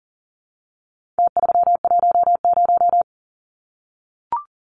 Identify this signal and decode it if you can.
Morse code, heard as a keyed tone, about 40 wpm, 700 Hz. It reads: T310
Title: Numbers - Contest Style - 4-Digit 40wpm